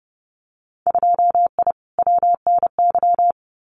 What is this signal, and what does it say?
Morse code, heard as a keyed tone, about 30 wpm, 700 Hz. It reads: 2I WNY
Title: Winter Field Day Exchanges 30wpm